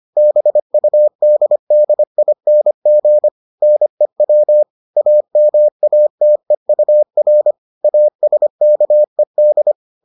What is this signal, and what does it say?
Morse code, heard as a keyed tone, about 25 wpm, 600 Hz. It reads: BUDDING NEW AMATEUR ASKED